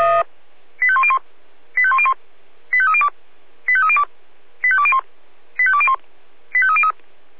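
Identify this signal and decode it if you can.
Five-tone selective call (ZVEI), heard as quick runs of five tones, tones of about 70 ms. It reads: 85101 85101 85302 85302 85201 85201 85303
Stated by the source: ZVEI five-tone selective call 85870 received with an SDR.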